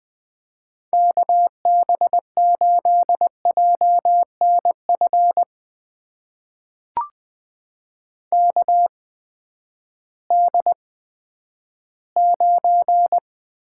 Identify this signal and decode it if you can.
Morse code, heard as a keyed tone, about 20 wpm, spaced as for 11 wpm, 700 Hz. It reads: KB8JNF KD9
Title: USA Call Signs - 6 Characters - 8x Character Spacing - 20wpm